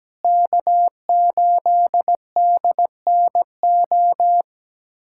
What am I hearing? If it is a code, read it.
Morse code, heard as a keyed tone, about 17 wpm, 700 Hz. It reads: K8DNO
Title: USA Callsign DE Callsign 17wpm